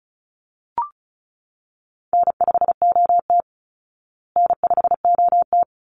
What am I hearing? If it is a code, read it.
Morse code, heard as a keyed tone, about 34 wpm, 700 Hz. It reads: N5OT N5OT
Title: Easy Call Signs Repeated 2x - 35wpm